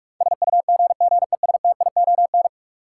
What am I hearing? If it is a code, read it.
Morse code, heard as a keyed tone, about 46 wpm, 700 Hz. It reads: SUGGESTION
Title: Sentences from Top 1600 Words - New Word Drill - 45wpm